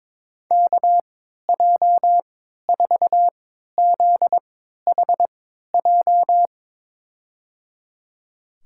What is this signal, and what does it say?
Morse code, heard as a keyed tone, about 22 wpm, spaced as for 12 wpm, 700 Hz. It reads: KJ4ZHJ
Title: USA Call Signs - 6 Characters - 3x Character Spacing - 22wpm